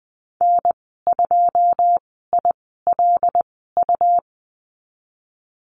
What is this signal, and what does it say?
Morse code, heard as a keyed tone, about 20 wpm, spaced as for 14 wpm, 700 Hz. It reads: N2ILU
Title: 154 Lesson 25g - Letter p - Call Signs 20wpm